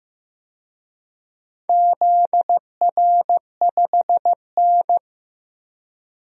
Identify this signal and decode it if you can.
Morse code, heard as a keyed tone, about 16 wpm, 700 Hz. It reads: ZR5N
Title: DXCC-100 Call Signs - 4 Characters - Repeated 3x - 15wpm